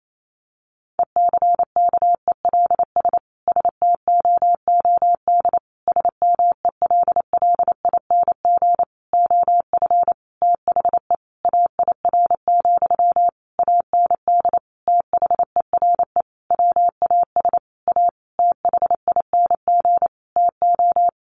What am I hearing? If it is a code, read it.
Morse code, heard as a keyed tone, about 28 wpm, 700 Hz. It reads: ECKELS STOOD SMELLING OF THE AIR, AND THERE WAS A THING TO
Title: A Sound Of Thunder 28wpm